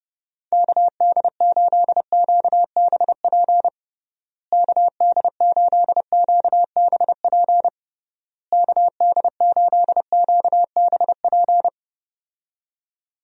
Morse code, heard as a keyed tone, about 30 wpm, 700 Hz. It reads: KD8QBP KD8QBP KD8QBP
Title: USA Call Signs - 6 Characters - Repeated 3x - 30wpm